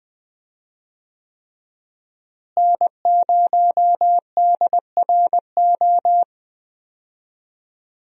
Morse code, heard as a keyed tone, about 21 wpm, 700 Hz. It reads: N0DRO
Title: USA Call Signs - 5 Characters - Repeated 3x - 20wpm